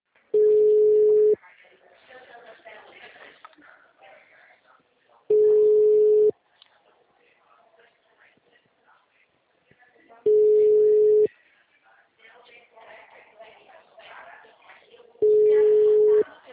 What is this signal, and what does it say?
Ringback tone (Europe, 425 Hz), with speech.